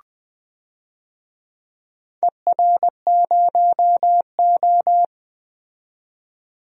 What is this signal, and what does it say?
Morse code, heard as a keyed tone, about 20 wpm, 700 Hz. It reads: ER0O
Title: DXCC-100 Call Signs - 4 Characters - 20wpm